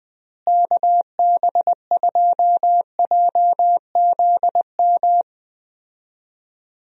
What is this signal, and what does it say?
Morse code, heard as a keyed tone, about 21 wpm, 700 Hz. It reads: KB2JZM
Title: USA Call Signs - 6 Characters - Repeated 3x - 20wpm